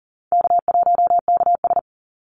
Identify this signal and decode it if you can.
Morse code, heard as a keyed tone, about 40 wpm, 700 Hz. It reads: K1KS
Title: Easy Call Signs Repeated 2x - 40wpm